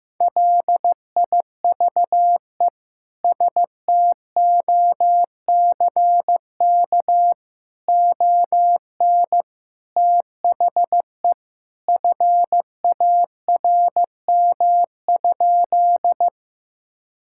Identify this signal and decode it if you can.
Morse code, heard as a keyed tone, about 15 wpm, 700 Hz. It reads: LIVE STOCK ON THE FARM?
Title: Sentences from Top 1000 Words - Review Part 5 - 15wpm